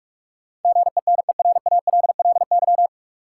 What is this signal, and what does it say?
Morse code, heard as a keyed tone, about 46 wpm, 700 Hz. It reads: GENERALLY
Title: Sentences from Top 600 Words - New Word Drill - 45wpm